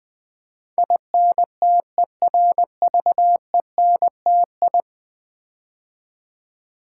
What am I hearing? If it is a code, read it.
Morse code, heard as a keyed tone, about 20 wpm, 700 Hz. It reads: INTERVENTI
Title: Build A Word - 5 Letters At A Time - 12-Letter Words - 20wpm